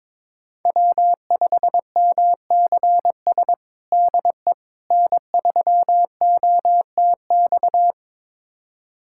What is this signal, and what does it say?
Morse code, heard as a keyed tone, about 22 wpm, 700 Hz. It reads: W5MCS DE N3OTX
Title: USA Callsign DE Callsign - 5 Characters - 22wpm